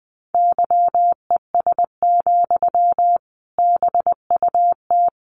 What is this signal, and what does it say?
Morse code, heard as a keyed tone, about 20 wpm, 700 Hz. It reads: YES, BUT
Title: Sentences from Top 1400 Words 20wpm